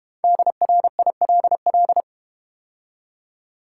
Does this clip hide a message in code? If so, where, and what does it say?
Yes: Morse code from 0.2 to 2.0 s, heard as a keyed tone, about 32 wpm, 700 Hz. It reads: DRILL